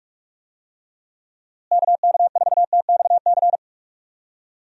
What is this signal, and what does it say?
Morse code, heard as a keyed tone, about 46 wpm, 700 Hz. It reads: KK4TXC
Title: USA Call Signs - 6 Characters - Repeated 3x - 45wpm